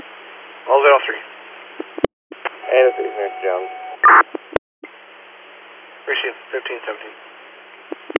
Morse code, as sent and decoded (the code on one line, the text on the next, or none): none